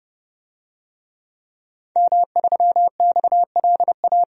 Morse code, heard as a keyed tone, about 29 wpm, 700 Hz. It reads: M3XLA
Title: DXCC-100 Call Signs - 5 Characters - 30wpm